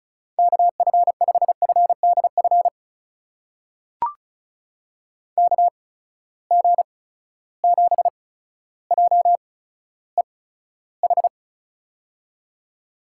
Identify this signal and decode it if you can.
Morse code, heard as a keyed tone, about 36 wpm, spaced as for 15 wpm, 700 Hz. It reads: KF5FDF KG7JEH
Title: USA Call Signs - 8x Character Spacing - 35wpm